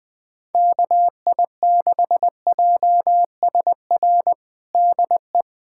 Morse code, heard as a keyed tone, about 21 wpm, 700 Hz. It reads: KI6JSR DE
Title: USA Callsign DE Callsign - 6 Characters - 20wpm